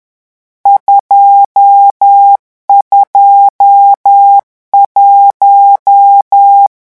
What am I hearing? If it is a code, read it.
Morse code, heard as a keyed tone, about 10 wpm, 800 Hz. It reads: 221
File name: morse mp3 / lecon10-vitesse 10